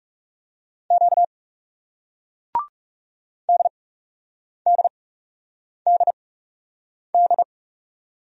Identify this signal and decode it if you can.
Morse code, heard as a keyed tone, about 41 wpm, spaced as for 8 wpm, 700 Hz. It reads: Q DDDD